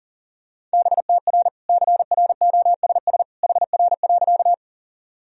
Morse code, heard as a keyed tone, about 40 wpm, 700 Hz. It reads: BTR CROSS HR.
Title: Sentences from Top 400 Words and Abbreviations 40wpm